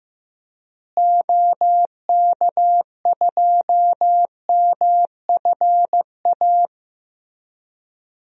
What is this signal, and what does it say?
Morse code, heard as a keyed tone, about 16 wpm, 700 Hz. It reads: OK2MFA
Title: DXCC-100 Call Signs - 6 Characters - Repeated 3x - 15wpm